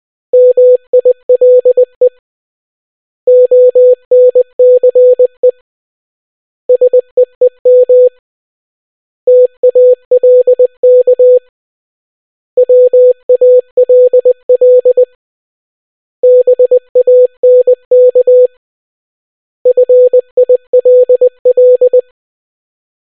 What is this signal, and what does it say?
Morse code, heard as a keyed tone, about 20 wpm, 500 Hz. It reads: MILE ONCE SEEM TALK WALL BANK FILL